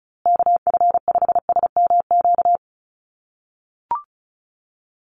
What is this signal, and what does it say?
Morse code, heard as a keyed tone, about 34 wpm, 700 Hz. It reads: KF5SMQ
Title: USA Call Signs Repeated 3x 35wpm